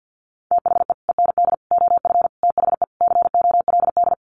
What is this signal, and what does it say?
Morse code, heard as a keyed tone, about 50 wpm, 700 Hz. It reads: THE END OF THE COLD
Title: Sets of 5 Words 50wpm